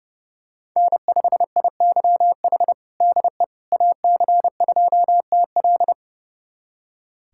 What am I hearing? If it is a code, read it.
Morse code, heard as a keyed tone, about 30 wpm, 700 Hz. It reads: N5IYH DE AC2TL
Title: USA Callsign DE Callsign - 5 Characters - 30wpm